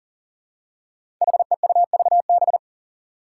Morse code, heard as a keyed tone, about 38 wpm, 700 Hz. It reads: HEUVB
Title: Five Challenging Letter-Number Combinations 40wpm